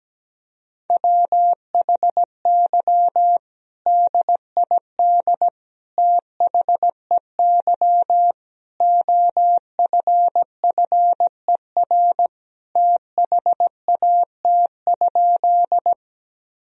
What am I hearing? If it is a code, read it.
Morse code, heard as a keyed tone, about 17 wpm, 700 Hz. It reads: WHY DID THEY OFFER THAT?
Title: Sentences from Top 1000 Words - Review Part 4 - 17wpm